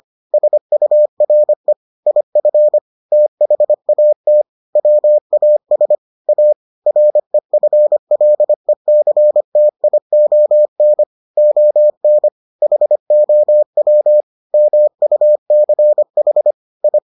Morse code, heard as a keyed tone, about 25 wpm, 600 Hz. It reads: SURE IF THAT WAS A REFLECTION ON HOW MUCH I